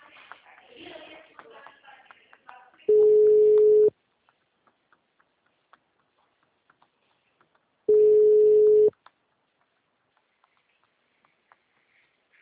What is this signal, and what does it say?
Ringback tone (Europe, 425 Hz), with speech.